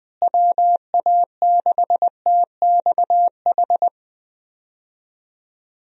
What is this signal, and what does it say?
Morse code, heard as a keyed tone, about 20 wpm, 700 Hz. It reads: WA6TXH
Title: USA Call Signs - 6 Characters - Repeated 2x - 20wpm